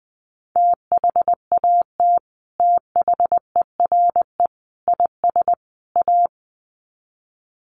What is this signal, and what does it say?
Morse code, heard as a keyed tone, about 20 wpm, 700 Hz. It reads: THAT THERE IS A
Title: Sets of 4 Words 20wpm